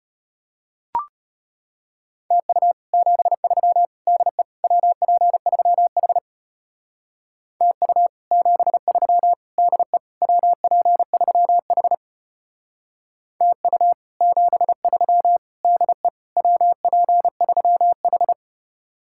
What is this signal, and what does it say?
Morse code, heard as a keyed tone, about 31 wpm, 700 Hz. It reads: TU 73 DE WP3H TU 73 DE WP3H TU 73 DE WP3H
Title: Speed-Racing Common QSO Phrases Encore - 25wpm